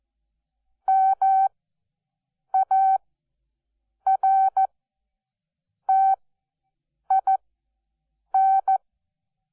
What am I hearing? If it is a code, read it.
Morse code, heard as a keyed tone, about 14 wpm, spaced as for 6 wpm, 770 Hz. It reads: MARTIN
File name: Onlinelehrgang / morsen / DJ4UF Morsen12